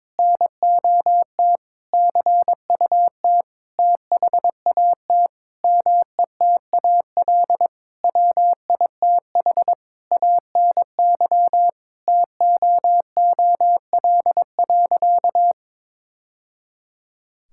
Morse code, heard as a keyed tone, about 22 wpm, 700 Hz. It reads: NOT CUT THAT METAL WITH ANY TOOL.
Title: Sentences from Top 1000 Words - Review Part 6 - 22wpm